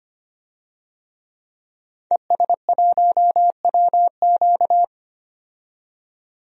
Morse code, heard as a keyed tone, about 25 wpm, 700 Hz. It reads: ES1WQ
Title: DXCC-100 Call Signs - 5 Characters - 25wpm